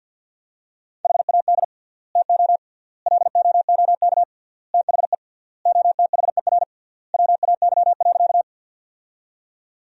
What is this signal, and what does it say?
Morse code, heard as a keyed tone, about 50 wpm, 700 Hz. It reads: HAD TO LOOK THE OTHER WAY.
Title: 3x Word-Spacing - Sentences from Top 100 Words - 50wpm